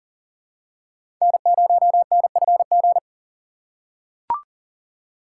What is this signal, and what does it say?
Morse code, heard as a keyed tone, about 40 wpm, 700 Hz. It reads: N0NFG
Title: USA Call Signs - 5 Characters - 6x Character Spacing - 40wpm